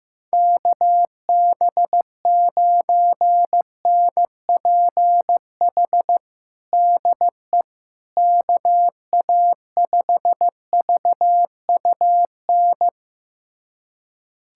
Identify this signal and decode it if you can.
Morse code, heard as a keyed tone, about 15 wpm, 700 Hz. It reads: KB9NPH DE KA5VUN
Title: USA Callsign DE Callsign 15wpm